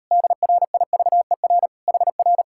Morse code, heard as a keyed tone, about 38 wpm, 700 Hz. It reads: DRIVER HR